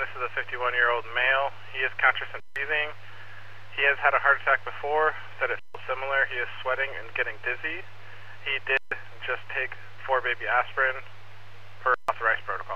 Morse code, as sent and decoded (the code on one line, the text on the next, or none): none